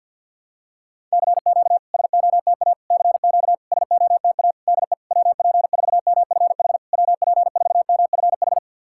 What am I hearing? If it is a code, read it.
Morse code, heard as a keyed tone, about 51 wpm, 700 Hz. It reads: CQ SOTA CQ SOTA DE WP4MFH WP4MFH
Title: SOTA QSO Practice 50wpm